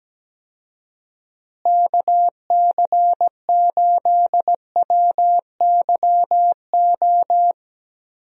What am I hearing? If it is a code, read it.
Morse code, heard as a keyed tone, about 17 wpm, 700 Hz. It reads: KC8WYO
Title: USA Call Signs - 6 Characters - Repeated 2x - 17wpm